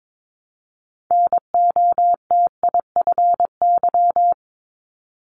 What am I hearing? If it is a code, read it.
Morse code, heard as a keyed tone, about 22 wpm, 700 Hz. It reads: NOTIFY
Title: Multi-letter Phonemes Repeated 3x 22wpm